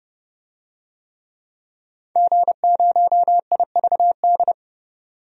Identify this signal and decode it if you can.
Morse code, heard as a keyed tone, about 30 wpm, 700 Hz. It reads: G0IVD
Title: DXCC-100 Call Signs - 5 Characters - 30wpm